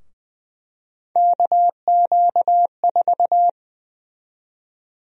Morse code, heard as a keyed tone, about 20 wpm, 700 Hz. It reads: KQ4
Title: USA Call Sign Prefixes - 20wpm